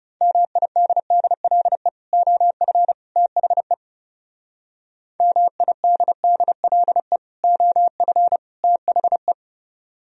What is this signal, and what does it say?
Morse code, heard as a keyed tone, about 36 wpm, 700 Hz. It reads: MIDDLE OF THE MIDDLE OF THE
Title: Speed-Racing - Sets of 3 Words 30wpm